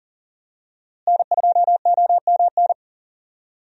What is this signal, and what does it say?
Morse code, heard as a keyed tone, about 40 wpm, 700 Hz. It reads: N2OMN
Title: USA Call Signs - 5 Characters - Repeated 3x - 40wpm